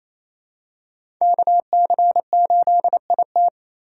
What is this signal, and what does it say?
Morse code, heard as a keyed tone, about 28 wpm, 700 Hz. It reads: KC8IT
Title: USA Call Signs - 4x Character Spacing - 28wpm